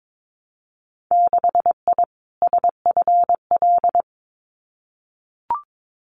Morse code, heard as a keyed tone, about 22 wpm, 700 Hz. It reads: 6I SFL
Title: Winter Field Day Exchanges Encore 22wpm